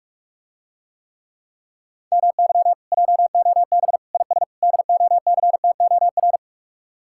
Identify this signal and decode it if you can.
Morse code, heard as a keyed tone, about 44 wpm, 700 Hz. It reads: MY JOB IS DOCTOR